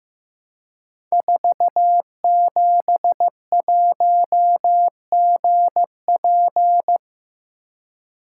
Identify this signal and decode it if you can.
Morse code, heard as a keyed tone, about 15 wpm, 700 Hz. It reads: 471GP
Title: Code Groups - 5 Letter:Numbers - 15wpm